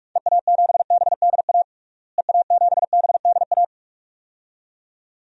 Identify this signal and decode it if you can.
Morse code, heard as a keyed tone, about 41 wpm, 690 Hz. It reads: EA7BDA EA7BDA
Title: DXCC-100 Call Signs - 6 Characters - Repeated 2x - 45wpm